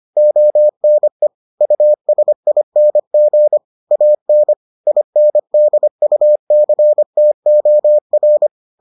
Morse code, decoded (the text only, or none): ONE USING AN INDUCTOR